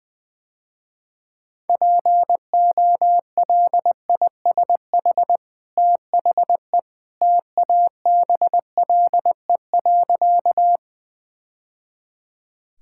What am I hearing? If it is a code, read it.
Morse code, heard as a keyed tone, about 21 wpm, 700 Hz. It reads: POLISH THE TABLE.
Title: Easy Sentences with Hints - 20wpm